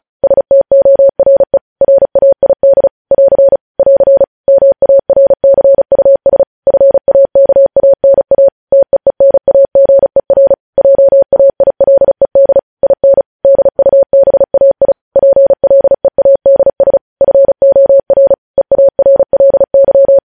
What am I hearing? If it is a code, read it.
Morse code, heard as a keyed tone, about 36 wpm, 550 Hz. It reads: STORE RAID <AR> <AR> MARCUS FAKANA TEENAGER JAILED IN DUBAI PLEADS FOR EARLY
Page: news headlines in morse code at 35 wpm